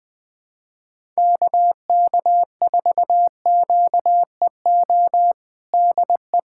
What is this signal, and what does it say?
Morse code, heard as a keyed tone, about 20 wpm, 700 Hz. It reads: KK4QEO DE
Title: USA Callsign DE Callsign 20wpm